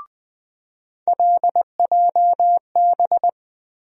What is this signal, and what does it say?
Morse code, heard as a keyed tone, about 20 wpm, 700 Hz. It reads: LJB
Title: USA Call Sign Postfixes - 3 Letters 20wpm